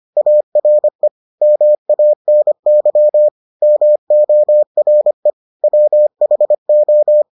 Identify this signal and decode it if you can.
Morse code, heard as a keyed tone, about 25 wpm, 600 Hz. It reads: ARE MANY MORE WHO